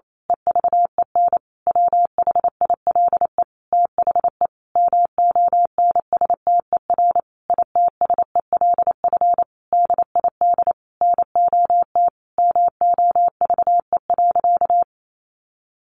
Morse code, heard as a keyed tone, about 28 wpm, 700 Hz. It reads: EVEN WHILE THE MONSTER ITSELF DID NOT MOVE.
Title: A Sound Of Thunder 28wpm